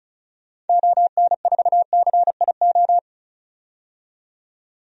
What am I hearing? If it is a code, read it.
Morse code, heard as a keyed tone, about 34 wpm, 700 Hz. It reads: ON4CIO
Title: DXCC-100 Call Signs - 6 Characters - 35wpm